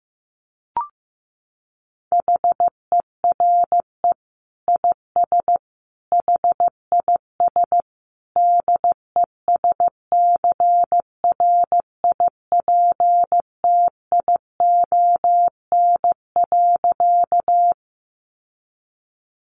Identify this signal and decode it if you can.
Morse code, heard as a keyed tone, about 15 wpm, 700 Hz. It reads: HERE IS HIS DESCRIPTION.
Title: Sentences from Top 1300 Words 15wpm